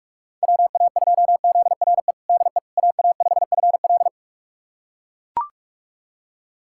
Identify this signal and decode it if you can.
Morse code, heard as a keyed tone, about 44 wpm, 690 Hz. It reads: WA2ZRE DE AA5FL
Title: USA Callsign DE Callsign 45wpm